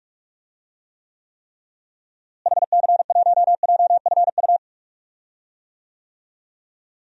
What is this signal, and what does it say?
Morse code, heard as a keyed tone, about 46 wpm, 700 Hz. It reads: HC1JFU
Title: DXCC-100 Call Signs - 6 Characters - Repeated 3x - 45wpm